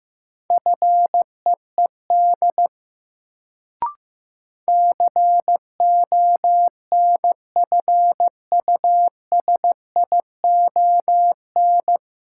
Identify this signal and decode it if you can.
Morse code, heard as a keyed tone, about 15 wpm, 700 Hz. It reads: FEED CONFUSION
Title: Sentences from Top 1600 Words - New Word Drill - 15wpm